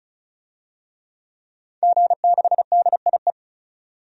Morse code, heard as a keyed tone, about 36 wpm, 700 Hz. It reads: G6DIE
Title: DXCC-100 Call Signs - 5 Characters - 35wpm